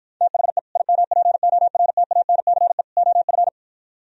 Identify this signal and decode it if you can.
Morse code, heard as a keyed tone, about 52 wpm, 700 Hz. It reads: THE IMPORTANCE OF